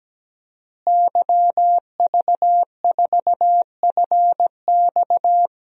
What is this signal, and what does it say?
Morse code, heard as a keyed tone, about 17 wpm, 700 Hz. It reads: YV4FX
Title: DXCC-100 Call Signs - 5 Characters - Repeated 3x - 17wpm